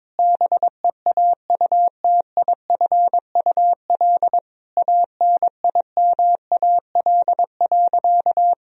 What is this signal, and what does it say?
Morse code, heard as a keyed tone, about 22 wpm, 700 Hz. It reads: BEAUTIFUL ANIMAL.
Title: Sentences from Top 1000 Words - Review Part 1 - 22wpm